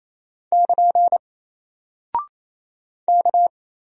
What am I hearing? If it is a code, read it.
Morse code, heard as a keyed tone, about 28 wpm, 700 Hz. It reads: <KN> K
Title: Single Character (Letter:Number:Punctuation:Prosign) 28wpm